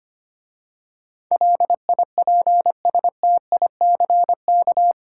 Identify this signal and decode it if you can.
Morse code, heard as a keyed tone, about 25 wpm, 700 Hz. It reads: LIPSTICK
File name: practice / 25wpm / Easy Words with Hints - 25wpm